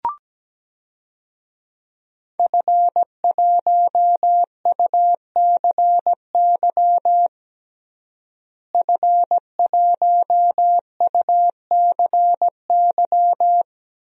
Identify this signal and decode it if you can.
Morse code, heard as a keyed tone, about 17 wpm, 700 Hz. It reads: F1UCY F1UCY
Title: DXCC-100 Call Signs - 5 Characters - Repeated 3x - 17wpm